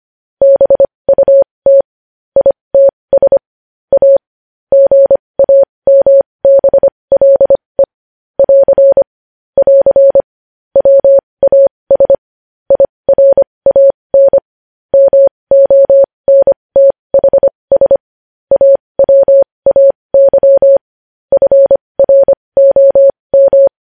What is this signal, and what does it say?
Morse code, heard as a keyed tone, about 25 wpm, spaced as for 20 wpm, 550 Hz. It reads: BUT ITS A GAMBLE <AR> <AR> WAS IRAN MONTHS AWAY FROM
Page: news headlines in morse code at 20 wpm